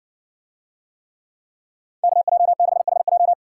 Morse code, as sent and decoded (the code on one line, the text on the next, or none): -.- .--- -.... .... .---
KJ6HJ